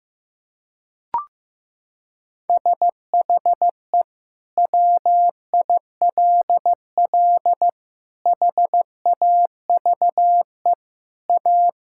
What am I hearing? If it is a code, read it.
Morse code, heard as a keyed tone, about 16 wpm, 700 Hz. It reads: SHE WILL HAVE A
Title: Sentences from Top 1000 Words - Review Part 5 - 15wpm